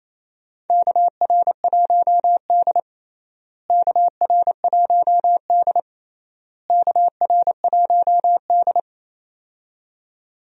Morse code, heard as a keyed tone, about 27 wpm, 700 Hz. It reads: KR1D KR1D KR1D
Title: USA Call Signs - 4 Characters - Repeated 3x - 28wpm